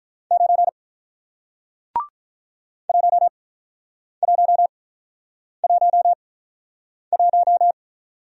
Morse code, heard as a keyed tone, about 48 wpm, spaced as for 9 wpm, 700 Hz. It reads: 9 1111